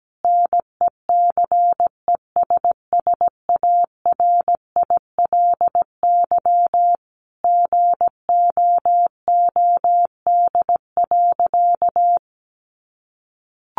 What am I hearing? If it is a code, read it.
Morse code, heard as a keyed tone, about 17 wpm, 700 Hz. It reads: NECESSARILY GOOD.
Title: Sentences from Top 1400 Words 17wpm